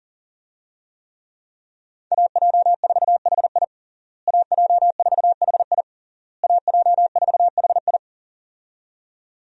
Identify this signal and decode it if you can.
Morse code, heard as a keyed tone, about 40 wpm, 700 Hz. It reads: AJ4HI AJ4HI AJ4HI
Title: USA Call Signs - 5 Characters - Repeated 3x - 40wpm